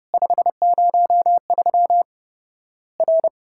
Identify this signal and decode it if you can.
Morse code, heard as a keyed tone, about 30 wpm, 700 Hz. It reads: 503 R
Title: Worked All Europe DX Contest - QTC Exchange - 30wpm